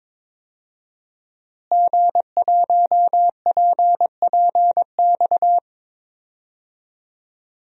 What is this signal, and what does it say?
Morse code, heard as a keyed tone, about 22 wpm, 700 Hz. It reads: G1PPX
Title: DXCC-100 Call Signs - 5 Characters - Repeated 2x - 22wpm